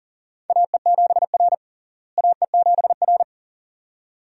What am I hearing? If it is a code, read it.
Morse code, heard as a keyed tone, about 37 wpm, 700 Hz. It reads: AE7R AE7R
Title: USA Call Signs - 4 Characters - Repeated 2x - 40wpm